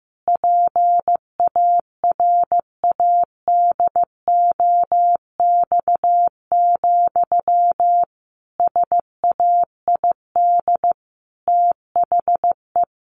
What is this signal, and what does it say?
Morse code, heard as a keyed tone, about 15 wpm, 700 Hz. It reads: PARADOX, SAID THE
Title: A Sound Of Thunder 15wpm